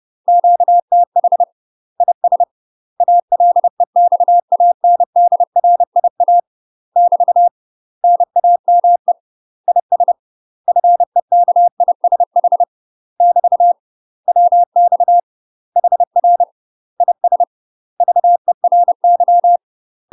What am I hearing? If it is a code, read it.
Morse code, heard as a keyed tone, about 30 wpm, 700 Hz. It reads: QTH IS ALEXANDRIA <BT> NAME IS FEKISH <BT> WX HR IS VERY
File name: UK MASTER QSOs TRAINING / 30WPM / UK MASTER QSOs 0007